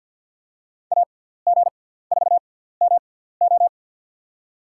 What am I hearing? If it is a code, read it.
Morse code, heard as a keyed tone, about 53 wpm, spaced as for 17 wpm, 700 Hz. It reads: AG4MO
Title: USA Call Signs - 5 Characters - 6x Character Spacing - 50wpm